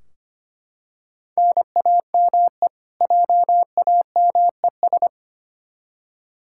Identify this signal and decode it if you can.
Morse code, heard as a keyed tone, about 25 wpm, 700 Hz. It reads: NAME JAMES
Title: Practice QSO - Exchange 1 - Beginning - 25wpm